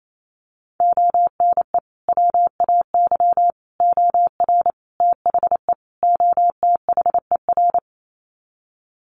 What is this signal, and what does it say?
Morse code, heard as a keyed tone, about 28 wpm, 700 Hz. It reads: ONE WAY OR THE OTHER
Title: Sets of 5 Words 28wpm